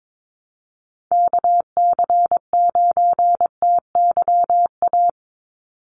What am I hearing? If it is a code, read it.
Morse code, heard as a keyed tone, about 22 wpm, 700 Hz. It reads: KC9TYA
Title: NA SOTA Chaser Callsigns 22wpm